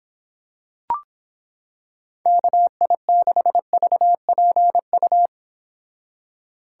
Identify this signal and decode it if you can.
Morse code, heard as a keyed tone, about 26 wpm, 700 Hz. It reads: KI6VPU